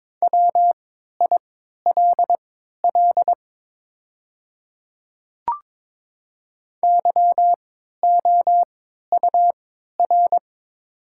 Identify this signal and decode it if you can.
Morse code, heard as a keyed tone, about 22 wpm, spaced as for 12 wpm, 690 Hz. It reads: WILL YOUR
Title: 3x Character Spacing - Top 200 Words - 22wpm